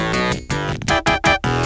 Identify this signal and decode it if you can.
There is no code.